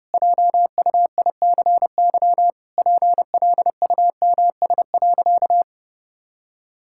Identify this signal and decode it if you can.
Morse code, heard as a keyed tone, about 32 wpm, 700 Hz. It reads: JUICY PLUMS.
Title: Pangram Challenge v2 - 30wpm